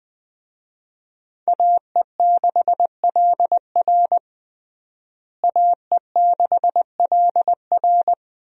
Morse code, heard as a keyed tone, about 21 wpm, 700 Hz. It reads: AE6LR AE6LR
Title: USA Call Signs - 5 Characters - Repeated 2x - 20wpm